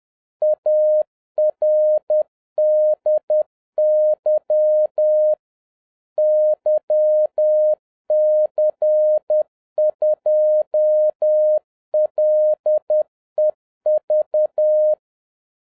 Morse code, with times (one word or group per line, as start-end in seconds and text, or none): 0.4-5.3 ARDY
6.2-14.9 YC2LEV